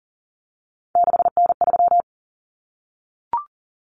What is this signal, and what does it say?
Morse code, heard as a keyed tone, about 38 wpm, 700 Hz. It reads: 6N3
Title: Common Cut Numbers - 3-Digit 40wpm